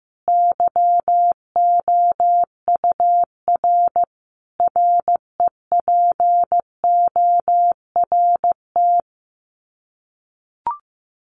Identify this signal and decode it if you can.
Morse code, heard as a keyed tone, about 15 wpm, 700 Hz. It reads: YOUR REPORT
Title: Common QSO Phrases 15wpm